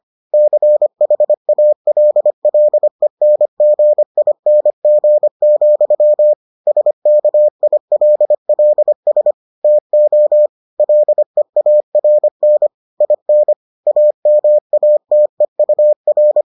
Morse code, heard as a keyed tone, about 25 wpm, 600 Hz. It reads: CHALLENGING, SKILLS TO LEARN IN AMATEUR